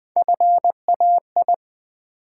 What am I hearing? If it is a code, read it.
Morse code, heard as a keyed tone, about 20 wpm, 700 Hz. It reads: FAI
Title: Build A Word - 1 Letter At A Time - 4-Letter Words - 20wpm